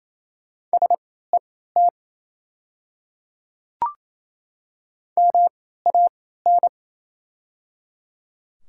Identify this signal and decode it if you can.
Morse code, heard as a keyed tone, about 28 wpm, spaced as for 16 wpm, 690 Hz. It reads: SET MAN
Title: 3x Character Spacing - Top 200 Words - 28wpm